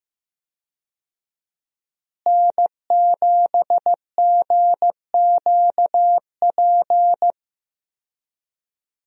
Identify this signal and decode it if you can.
Morse code, heard as a keyed tone, about 15 wpm, 700 Hz. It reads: N7GQP
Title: USA Call Signs - 5 Characters - Repeated 3x - 15wpm